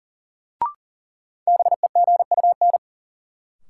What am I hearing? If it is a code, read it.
Morse code, heard as a keyed tone, about 37 wpm, 700 Hz. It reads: BEGUN
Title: Sentences from Top 1000 Words - Review Part 6 - New Word Drill - 40wpm